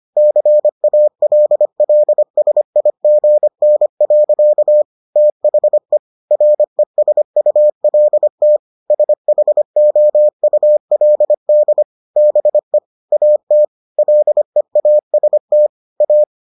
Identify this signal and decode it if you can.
Morse code, heard as a keyed tone, about 25 wpm, 600 Hz. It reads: CALLSIGN. THE RESULT SHOULD BE AT LEAST A